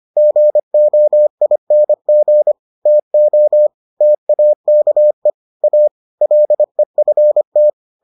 Morse code, heard as a keyed tone, about 25 wpm, 600 Hz. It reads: GOING TO TAKE A LEFT